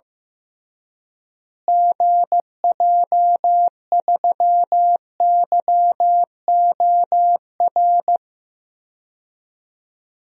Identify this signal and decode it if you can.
Morse code, heard as a keyed tone, about 15 wpm, 700 Hz. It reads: GJ3YOR
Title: DXCC-100 Call Signs - 6 Characters - Repeated 3x - 15wpm